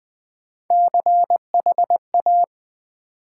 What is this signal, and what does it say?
Morse code, heard as a keyed tone, about 20 wpm, 700 Hz. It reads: CHA
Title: Build A Word - 3 Letters At A Time - 8-Letter Words - 20wpm